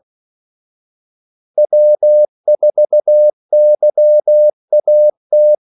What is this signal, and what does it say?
Morse code, heard as a keyed tone, about 16 wpm, 600 Hz. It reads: W4YAT
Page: Kaldesignaler 16 wpm